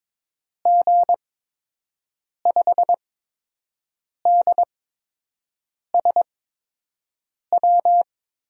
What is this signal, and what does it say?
Morse code, heard as a keyed tone, about 22 wpm, spaced as for 6 wpm, 700 Hz. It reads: G5DSW
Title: USA Call Signs - 6 Characters - 8x Character Spacing - 22wpm